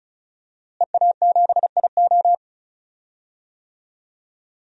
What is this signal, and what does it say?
Morse code, heard as a keyed tone, about 34 wpm, 700 Hz. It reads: EA7IO
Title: DXCC-100 Call Signs - 5 Characters - Repeated 3x - 35wpm